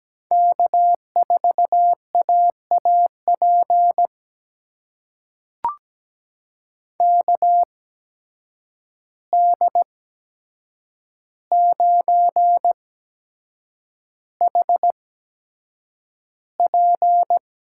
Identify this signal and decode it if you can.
Morse code, heard as a keyed tone, about 17 wpm, spaced as for 7 wpm, 700 Hz. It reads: K4AAP KD9HP
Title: USA Call Signs - 5 Characters - 8x Character Spacing - 17wpm